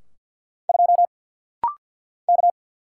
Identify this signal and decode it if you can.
Morse code, heard as a keyed tone, about 50 wpm, 700 Hz. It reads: 2 K
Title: Single Letter-Number - Rapid-Fire 50wpm